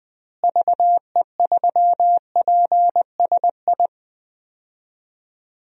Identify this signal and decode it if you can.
Morse code, heard as a keyed tone, about 21 wpm, 700 Hz. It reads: VE3PSI
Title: DXCC-100 Call Signs - 6 Characters - Repeated 2x - 20wpm